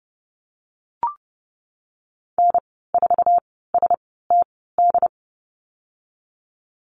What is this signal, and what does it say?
Morse code, heard as a keyed tone, about 30 wpm, spaced as for 17 wpm, 700 Hz. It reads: N4STD
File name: Lesson 14g - Number 2 - Call Signs 30wpm